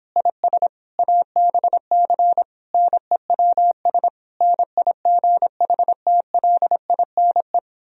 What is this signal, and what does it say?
Morse code, heard as a keyed tone, about 26 wpm, 700 Hz. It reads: IS ABC NEWS NIGHTLINE